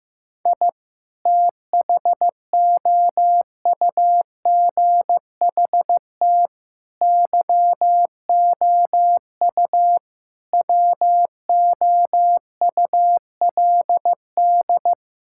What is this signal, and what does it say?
Morse code, heard as a keyed tone, about 15 wpm, 700 Hz. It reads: I THOUGHT YOU WOULD
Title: Sentences from Top 1000 Words - Review Part 5 - 15wpm